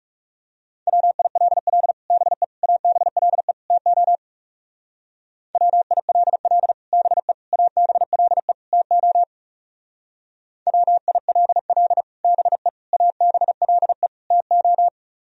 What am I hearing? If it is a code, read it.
Morse code, heard as a keyed tone, about 36 wpm, 700 Hz. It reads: WILL BE ABLE TO WILL BE ABLE TO WILL BE ABLE TO
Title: Speed-Racing - Sets of 4 Words 30wpm